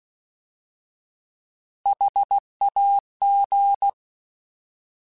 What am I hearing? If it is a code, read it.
Morse code, heard as a keyed tone, about 16 wpm, 800 Hz. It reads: HAG
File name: Alphabetical / Alphabetical mp3s / Hs @15